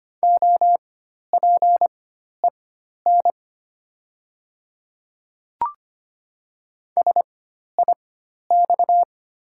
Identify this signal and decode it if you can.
Morse code, heard as a keyed tone, about 25 wpm, spaced as for 12 wpm, 700 Hz. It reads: OPEN SIX
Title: 4x Character Spacing - Top 500 Words - 25wpm